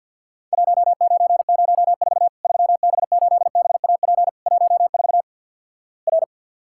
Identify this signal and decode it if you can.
Morse code, heard as a keyed tone, about 48 wpm, 700 Hz. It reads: 1904 3B8BAP 14 R
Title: Worked All Europe DX Contest - QTC Exchange - 50wpm